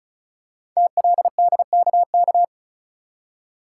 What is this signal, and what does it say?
Morse code, heard as a keyed tone, about 36 wpm, 690 Hz. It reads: TLDKK
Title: Code Groups - 5 Letters - 35wpm